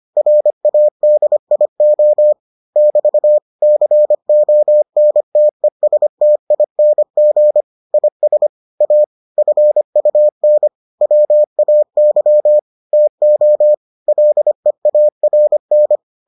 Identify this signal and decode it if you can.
Morse code, heard as a keyed tone, about 25 wpm, 600 Hz. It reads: RADIO <BT> CONTESTING IS A FUN WAY TO LEARN